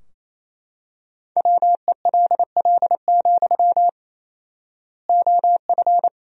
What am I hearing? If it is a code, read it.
Morse code, heard as a keyed tone, about 28 wpm, 700 Hz. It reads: WELL, OF
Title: 4x Word-Spacing - Sentences from Top 500 Words - 28wpm